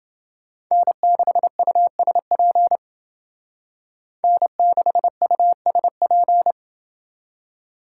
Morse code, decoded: N6USP N6USP

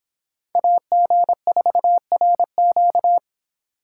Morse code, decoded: AG4RQ

